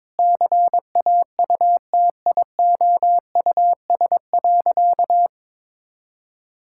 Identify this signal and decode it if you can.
Morse code, heard as a keyed tone, about 22 wpm, 700 Hz. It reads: CAUTIOUS.
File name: practice / 22wpm / Sentences with 8-Word Phrases - 22wpm